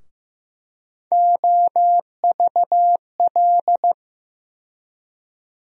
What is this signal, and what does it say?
Morse code, heard as a keyed tone, about 15 wpm, 700 Hz. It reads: OVL
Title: USA Call Sign Postfixes - 3 Letters 15wpm